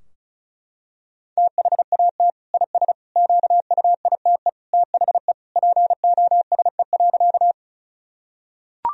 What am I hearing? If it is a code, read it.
Morse code, heard as a keyed tone, about 34 wpm, 700 Hz. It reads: THAT IS QUITE THE POSE.
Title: Sentences from Top 1000 Words - Review Part 2 - 35wpm